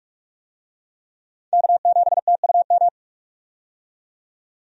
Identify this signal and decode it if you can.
Morse code, heard as a keyed tone, about 46 wpm, 700 Hz. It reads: K7TUM
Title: USA Call Signs - 5 Characters - 45wpm